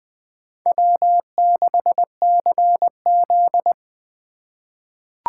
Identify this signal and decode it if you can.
Morse code, heard as a keyed tone, about 20 wpm, 700 Hz. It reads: W6CZ
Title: USA Call Signs - 6x Character Spacing - 20wpm